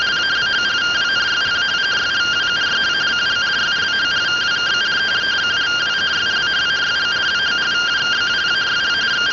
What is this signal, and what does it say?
Two-tone teleprinter signal, about 170 Hz shift.